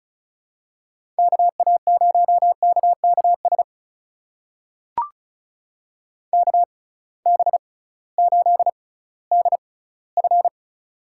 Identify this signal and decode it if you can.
Morse code, heard as a keyed tone, about 36 wpm, spaced as for 19 wpm, 700 Hz. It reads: KA0KKS KB8DF
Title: USA Call Signs - 6 Characters - 6x Character Spacing - 35wpm